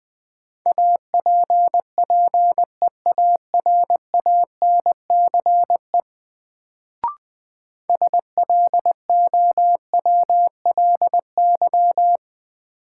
Morse code, heard as a keyed tone, about 20 wpm, 700 Hz. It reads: APPEARANCE SLOWLY
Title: Sentences from Top 1100 Words - New Word Drill Alt - 20wpm